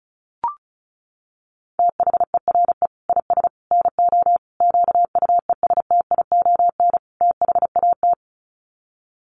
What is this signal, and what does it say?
Morse code, heard as a keyed tone, about 34 wpm, 700 Hz. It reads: THERE IS NO QUESTION THAT
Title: Sets of 5 Words Encore 35wpm